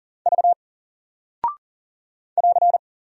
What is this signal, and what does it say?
Morse code, heard as a keyed tone, about 40 wpm, 700 Hz. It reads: V <AR>
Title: Single Character (Letter:Number:Punctuation:Prosign) 40wpm